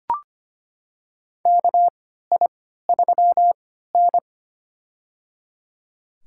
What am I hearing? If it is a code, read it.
Morse code, heard as a keyed tone, about 25 wpm, spaced as for 14 wpm, 700 Hz. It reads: KI3N
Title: USA Call Signs - 4 Characters - 3x Character Spacing - 25wpm